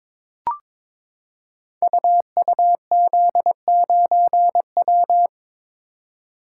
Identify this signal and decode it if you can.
Morse code, heard as a keyed tone, about 22 wpm, 690 Hz. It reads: UUZ9W
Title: Code Groups - 5 Letter:Numbers - 22wpm